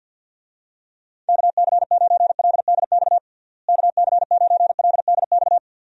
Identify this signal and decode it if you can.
Morse code, heard as a keyed tone, about 52 wpm, 700 Hz. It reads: KC9LDX KC9LDX
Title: USA Call Signs - 6 Characters - Repeated 3x - 50wpm